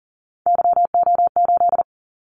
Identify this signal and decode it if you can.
Morse code, heard as a keyed tone, about 40 wpm, 700 Hz. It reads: YO8
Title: Callsign Prefixes for Top 25 DXCC - 40wpm